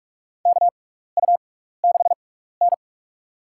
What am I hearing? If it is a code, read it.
Morse code, heard as a keyed tone, about 46 wpm, spaced as for 15 wpm, 700 Hz. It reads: KU6N
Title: USA Call Signs - 4 Characters - 6x Character Spacing - 45wpm